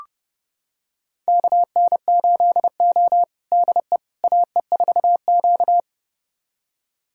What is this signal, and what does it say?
Morse code, heard as a keyed tone, about 32 wpm, 700 Hz. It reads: KN8O DE AE4Q
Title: USA Callsign DE Callsign - 4 Characters - 30wpm